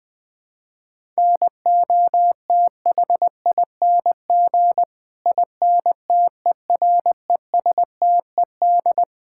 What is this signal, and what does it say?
Morse code, heard as a keyed tone, about 21 wpm, 700 Hz. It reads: NOTHING INTERESTED
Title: Sentences from Top 1000 Words - Review Part 6 - 20wpm